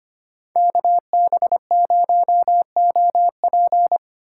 Morse code, heard as a keyed tone, about 32 wpm, 700 Hz. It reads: KB0OP